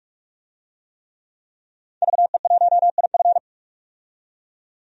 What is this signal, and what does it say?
Morse code, heard as a keyed tone, about 46 wpm, 700 Hz. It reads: VE1IF
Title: DXCC-100 Call Signs - 5 Characters - 45wpm